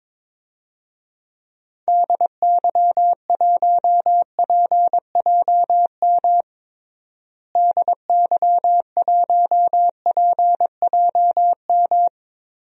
Morse code, heard as a keyed tone, about 22 wpm, 700 Hz. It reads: DY1PJM DY1PJM
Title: DXCC-100 Call Signs - 6 Characters - Repeated 3x - 22wpm